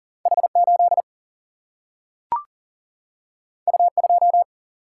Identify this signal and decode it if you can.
Morse code, heard as a keyed tone, about 39 wpm, 700 Hz. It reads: H8 U2
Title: Single-Letter Single-Number 40wpm